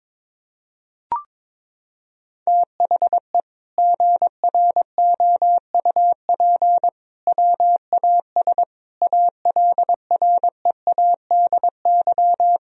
Morse code, heard as a keyed tone, about 22 wpm, 700 Hz. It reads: THE GROUP WAS ALREADY
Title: Sentences from Top 1000 Words - Review Part 1 - 22wpm